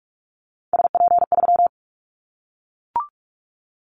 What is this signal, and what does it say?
Morse code, heard as a keyed tone, about 44 wpm, 700 Hz. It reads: SP3
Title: Callsign Prefixes for Top 25 DXCC - 45wpm